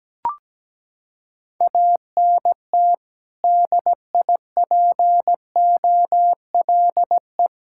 Morse code, heard as a keyed tone, about 17 wpm, 700 Hz. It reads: ANT DIPOLE
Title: K1USN SST QSO Practice 17wpm